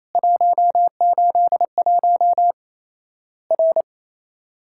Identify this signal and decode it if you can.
Morse code, heard as a keyed tone, about 27 wpm, 700 Hz. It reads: 181 R
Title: Worked All Europe DX Contest - QTC Exchange - 28wpm